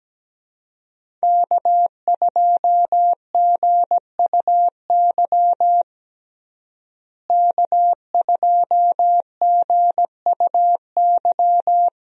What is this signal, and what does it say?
Morse code, heard as a keyed tone, about 17 wpm, 700 Hz. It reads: K2GUY K2GUY
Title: USA Call Signs - 5 Characters - Repeated 3x - 17wpm